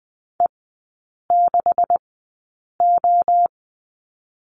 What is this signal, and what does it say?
Morse code, heard as a keyed tone, about 20 wpm, spaced as for 8 wpm, 700 Hz. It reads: E6O
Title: 159 Lesson 26e - Letter g - Three Characters 20wpm